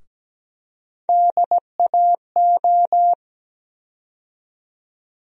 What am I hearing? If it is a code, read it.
Morse code, heard as a keyed tone, about 17 wpm, 700 Hz. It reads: DAO
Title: USA Call Sign Postfixes - 3 Letters 17wpm